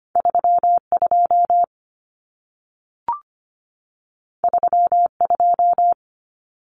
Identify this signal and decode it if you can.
Morse code, heard as a keyed tone, about 25 wpm, 700 Hz. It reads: 32 32
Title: Numbers - 2-Digit 25wpm